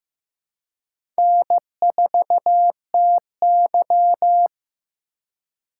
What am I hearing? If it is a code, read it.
Morse code, heard as a keyed tone, about 15 wpm, 700 Hz. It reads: N4TY
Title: USA Call Signs - 4x Character Spacing - 15wpm